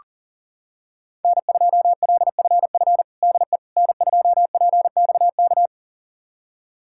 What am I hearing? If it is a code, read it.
Morse code, heard as a keyed tone, about 40 wpm, 700 Hz. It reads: N2LFF DE N2PXK
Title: USA Callsign DE Callsign - 5 Characters - 40wpm